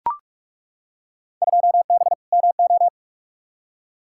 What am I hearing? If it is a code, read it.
Morse code, heard as a keyed tone, about 42 wpm, 700 Hz. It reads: 2B MO
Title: Field Day Exchanges - 45wpm